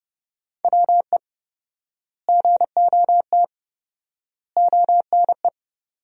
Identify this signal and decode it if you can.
Morse code, heard as a keyed tone, about 29 wpm, 700 Hz. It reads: WE GOT ONE
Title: 4x Word-Spacing - Sentences from Top 300 Words - 30wpm